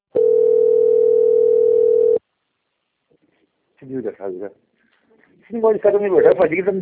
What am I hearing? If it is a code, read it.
Ringback tone (North America), 440 + 480 Hz, with speech.
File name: out- - - - - .21228